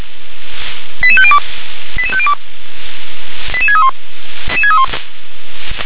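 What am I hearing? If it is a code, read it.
Five-tone selective call (ZVEI), heard as quick runs of five tones, tones of about 70 ms. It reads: 88502 88502 88631
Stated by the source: ZVEI five-tone selective call received on SDR.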